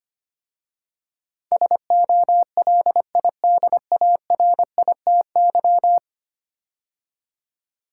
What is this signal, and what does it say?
Morse code, heard as a keyed tone, about 25 wpm, 700 Hz. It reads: SOLIDARITY
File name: 10 Letter Words - First 3 Letters Given - 25wpm